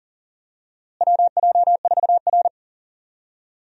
Morse code, heard as a keyed tone, about 37 wpm, 700 Hz. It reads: WJ4R
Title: USA Call Signs - 4 Characters - Repeated 3x - 40wpm